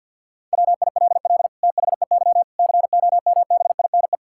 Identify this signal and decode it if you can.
Morse code, heard as a keyed tone, about 50 wpm, 700 Hz. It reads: WILL THEY COMBINE
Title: Sentences from Top 1600 Words 50wpm